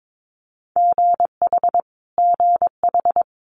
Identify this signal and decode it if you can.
Morse code, heard as a keyed tone, about 22 wpm, 700 Hz. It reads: GH GH
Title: Multi-letter Phonemes Repeated 3x 22wpm